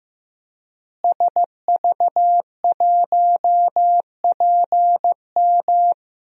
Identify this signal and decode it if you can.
Morse code, heard as a keyed tone, about 15 wpm, 700 Hz. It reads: SV1PM
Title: DXCC-100 Call Signs - 5 Characters - Repeated 3x - 15wpm